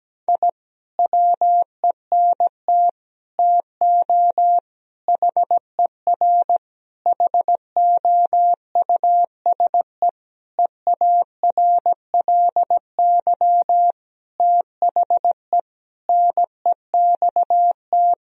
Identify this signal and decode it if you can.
Morse code, heard as a keyed tone, about 17 wpm, 700 Hz. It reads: I WENT TO HER HOUSE EARLY THE NEXT
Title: Sentences from Top 1000 Words - Review Part 2 - 17wpm